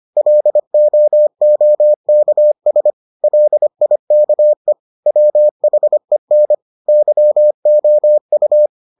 Morse code, heard as a keyed tone, about 25 wpm, 600 Hz. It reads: LOOKS LIKE WHEN YOU